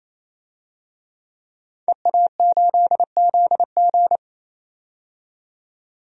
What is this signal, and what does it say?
Morse code, heard as a keyed tone, about 27 wpm, 700 Hz. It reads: EA8ZG
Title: DXCC-100 Call Signs - 5 Characters - 28wpm